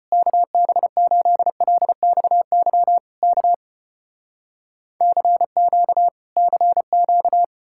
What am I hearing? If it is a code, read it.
Morse code, heard as a keyed tone, about 35 wpm, 700 Hz. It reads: KB8LXY K CQ CQ